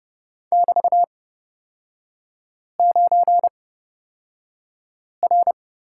Morse code, heard as a keyed tone, about 31 wpm, spaced as for 5 wpm, 700 Hz. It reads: <BT>9R
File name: Lesson 36e - BT - Three Characters 30wpm